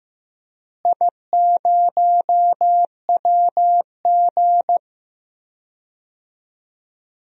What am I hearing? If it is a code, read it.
Morse code, heard as a keyed tone, about 15 wpm, 700 Hz. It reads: I0WG
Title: DXCC-100 Call Signs - 4 Characters - Repeated 2x - 15wpm